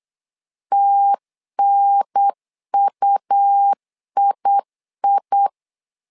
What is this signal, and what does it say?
Morse code, heard as a keyed tone, about 8 wpm, 800 Hz. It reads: TNUII
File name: LEZIONI / 3 MWRO / 3 M 8wpm 2300